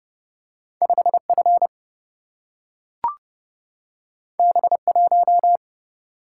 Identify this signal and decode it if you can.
Morse code, heard as a keyed tone, about 30 wpm, 700 Hz. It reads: 5F B1
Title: Single-Letter Single-Number 30wpm